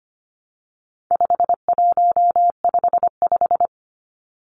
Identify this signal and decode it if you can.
Morse code, heard as a keyed tone, about 25 wpm, 700 Hz. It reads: 5155
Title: Numbers - 4-Digit 25wpm